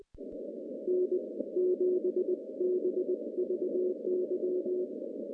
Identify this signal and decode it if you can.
Morse code, heard as a keyed tone, about 20 wpm, spaced as for 17 wpm, 400 Hz, with noise. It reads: N7BVY
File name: CW-SS-k5tr